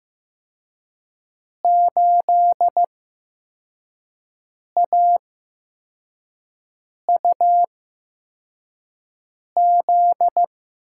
Morse code, heard as a keyed tone, about 15 wpm, spaced as for 4 wpm, 700 Hz. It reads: 8AUZ
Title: USA Call Signs - 8x Character Spacing - 15wpm